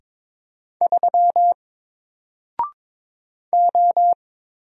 Morse code, heard as a keyed tone, about 22 wpm, 700 Hz. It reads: 3 O
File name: Single-Letter-Number-22wpm